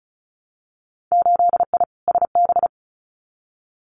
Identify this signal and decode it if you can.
Morse code, heard as a keyed tone, about 36 wpm, 700 Hz. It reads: 8I SB
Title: Winter Field Day Exchanges Encore 35wpm